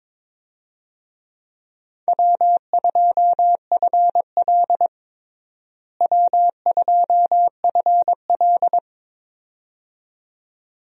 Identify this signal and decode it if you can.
Morse code, heard as a keyed tone, about 22 wpm, 700 Hz. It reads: W2FL W2FL
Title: USA Call Signs - 4 Characters - Repeated 2x - 22wpm